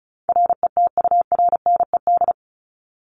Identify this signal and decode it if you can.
Morse code, heard as a keyed tone, about 34 wpm, 700 Hz. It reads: RETURNED
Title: ED Suffix - Top 100 Words with Base Words 35wpm